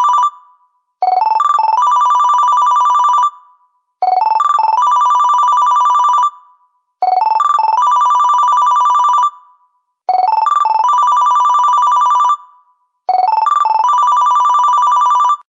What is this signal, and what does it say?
There is no code.